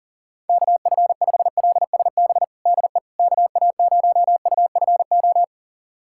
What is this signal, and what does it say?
Morse code, heard as a keyed tone, about 40 wpm, 700 Hz. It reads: KF5LSB DE KA0UFO
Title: USA Callsign DE Callsign 40wpm